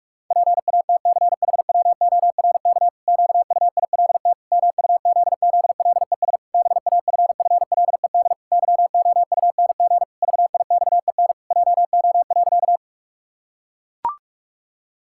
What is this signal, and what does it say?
Morse code, heard as a keyed tone, about 41 wpm, 700 Hz. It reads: PATCHWORK QUILT MUZZLES BAFFLED YOUNG VIXEN JO.
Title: Pangram Challenge v2 - 45wpm